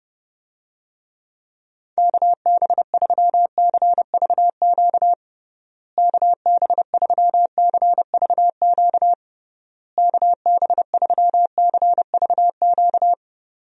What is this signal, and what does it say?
Morse code, heard as a keyed tone, about 31 wpm, 700 Hz. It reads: KB3CVQ KB3CVQ KB3CVQ
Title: USA Call Signs - 6 Characters - Repeated 3x - 30wpm